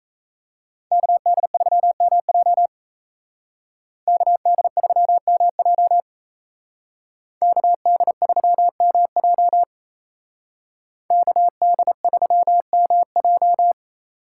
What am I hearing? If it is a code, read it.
Morse code, heard as a keyed tone, about 32 wpm, 700 Hz. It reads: KD3MJ KD3MJ KD3MJ KD3MJ
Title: Speed Racing - USA 2x2 Call Signs 28wpm